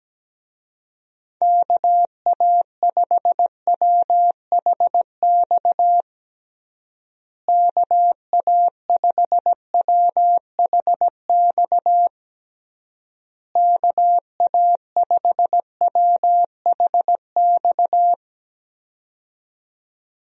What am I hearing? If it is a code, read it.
Morse code, heard as a keyed tone, about 17 wpm, 700 Hz. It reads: KA5WHX KA5WHX KA5WHX
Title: USA Call Signs - 6 Characters - Repeated 3x - 17wpm